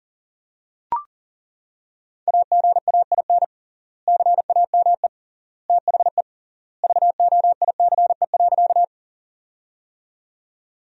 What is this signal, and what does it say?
Morse code, heard as a keyed tone, about 40 wpm, 700 Hz. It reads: AGAIN CAME THE VOICE.
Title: 3x Word-Spacing - Sentences from Top 500 Words - 40wpm